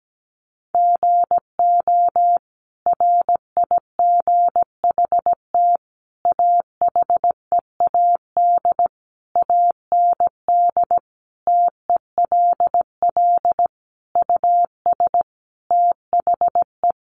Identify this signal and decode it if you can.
Morse code, heard as a keyed tone, about 17 wpm, 700 Hz. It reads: GO RIGHT AHEAD AND TELL US THE
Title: Sentences from Top 1100 Words 17wpm